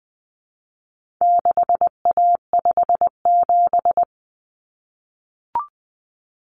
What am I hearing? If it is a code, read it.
Morse code, heard as a keyed tone, about 20 wpm, 700 Hz. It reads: 6A57
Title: Common Cut Numbers - 4-Digit 20wpm